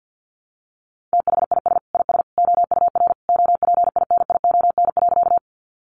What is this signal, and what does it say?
Morse code, heard as a keyed tone, about 50 wpm, 700 Hz. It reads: THIS IS OUR OPINION.
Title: Sentences from Top 1200 Words 50wpm